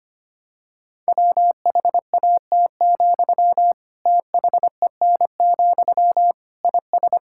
Morse code, heard as a keyed tone, about 25 wpm, 700 Hz. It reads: WHAT, THEN, IS
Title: Sentences from Top 1000 Words - Review Part 6 - 25wpm